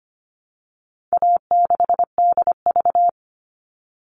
Morse code, heard as a keyed tone, about 25 wpm, 700 Hz. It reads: A6DV
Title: All Cut Numbers - 4-Digit 25wpm